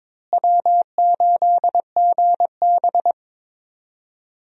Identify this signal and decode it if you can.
Morse code, heard as a keyed tone, about 22 wpm, 700 Hz. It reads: W8GB
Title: USA Call Signs - 4 Characters - Repeated 2x - 22wpm